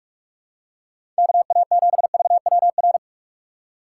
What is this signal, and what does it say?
Morse code, heard as a keyed tone, about 44 wpm, 700 Hz. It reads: KA7VWR
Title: USA Call Signs - 6 Characters - 6x Character Spacing - 45wpm